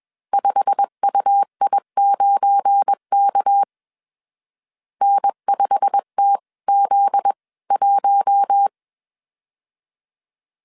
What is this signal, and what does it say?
Morse code, heard as a keyed tone, about 21 wpm, spaced as for 19 wpm, 800 Hz. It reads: 5UI9K N5TZ1